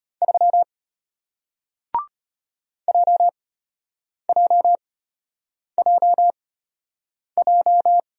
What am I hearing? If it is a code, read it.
Morse code, heard as a keyed tone, about 33 wpm, spaced as for 8 wpm, 700 Hz. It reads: 3 JJJJ